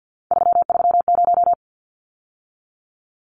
Morse code, heard as a keyed tone, about 49 wpm, 700 Hz. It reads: 330
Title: Numbers - 3-Digit 50wpm